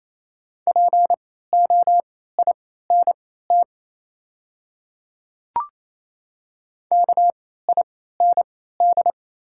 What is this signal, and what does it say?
Morse code, heard as a keyed tone, about 28 wpm, spaced as for 16 wpm, 700 Hz. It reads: POINT KIND
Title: 3x Character Spacing - Top 300 Words - 28wpm